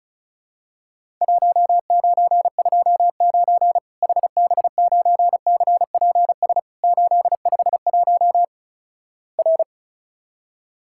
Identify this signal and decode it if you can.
Morse code, heard as a keyed tone, about 36 wpm, 700 Hz. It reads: 1929 HB9CPS 851 R
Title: Worked All Europe DX Contest - QTC Exchange - 35wpm